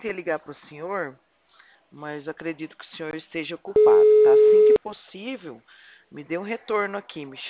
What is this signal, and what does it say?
Ringback tone (Europe, 425 Hz), with speech.
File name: out- -2018- - - .14406